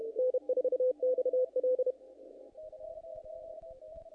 Morse code, heard as a keyed tone, about 32 wpm, 510 Hz. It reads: N4XL
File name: contests / naqp-cw-k5tr-op-k5pi- -2000